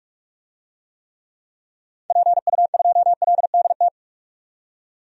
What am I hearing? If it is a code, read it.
Morse code, heard as a keyed tone, about 45 wpm, 700 Hz. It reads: PU2LDT
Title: DXCC-100 Call Signs - 6 Characters - 45wpm